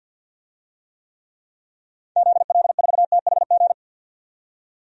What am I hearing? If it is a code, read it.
Morse code, heard as a keyed tone, about 53 wpm, 700 Hz. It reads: ZL4THG
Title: DXCC-100 Call Signs - 6 Characters - 50wpm